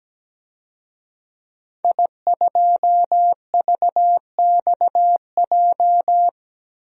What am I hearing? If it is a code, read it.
Morse code, heard as a keyed tone, about 17 wpm, 700 Hz. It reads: I2VXJ
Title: DXCC-100 Call Signs - 5 Characters - 17wpm